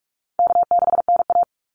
Morse code, heard as a keyed tone, about 44 wpm, 700 Hz. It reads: K6NA
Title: Easy Call Signs Repeated 3x - 45wpm